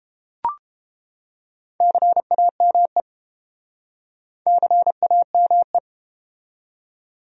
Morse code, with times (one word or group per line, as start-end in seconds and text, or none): 1.8-3.0 CAME
4.5-5.8 CAME